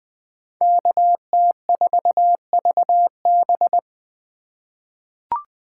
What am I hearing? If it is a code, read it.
Morse code, heard as a keyed tone, about 20 wpm, 700 Hz. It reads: KT4VB
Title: USA Call Signs - 5 Characters - 8x Character Spacing - 20wpm